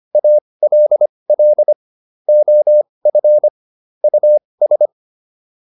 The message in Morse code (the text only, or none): ALL OF US